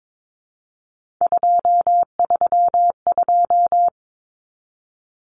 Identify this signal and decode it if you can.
Morse code, heard as a keyed tone, about 22 wpm, 700 Hz. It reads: 232
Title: Numbers - 3-Digit 22wpm